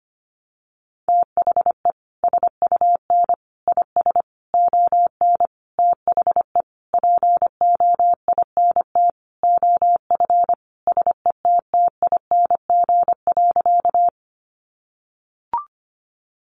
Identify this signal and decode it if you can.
Morse code, heard as a keyed tone, about 25 wpm, 700 Hz. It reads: THE SUN IS ON THE POINT OF SETTING.
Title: Sentences from Top 1300 Words 25wpm